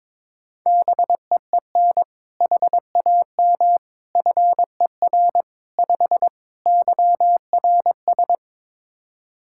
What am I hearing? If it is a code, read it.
Morse code, heard as a keyed tone, about 22 wpm, 700 Hz. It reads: BEEN HAM FER 5 YRS
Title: K1USN SST QSO Practice 22wpm